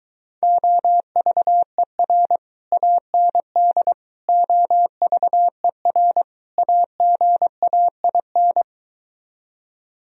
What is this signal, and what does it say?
Morse code, heard as a keyed tone, about 23 wpm, 700 Hz. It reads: OVER AND OVER AGAIN